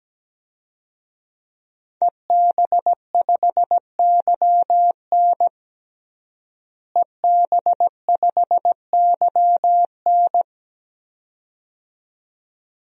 Morse code, heard as a keyed tone, about 17 wpm, 700 Hz. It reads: EB5YN EB5YN
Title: DXCC-100 Call Signs - 5 Characters - Repeated 2x - 17wpm